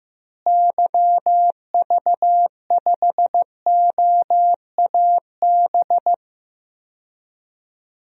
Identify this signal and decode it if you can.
Morse code, heard as a keyed tone, about 15 wpm, 700 Hz. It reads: YV5OAB
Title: DXCC-100 Call Signs - 6 Characters - Repeated 3x - 15wpm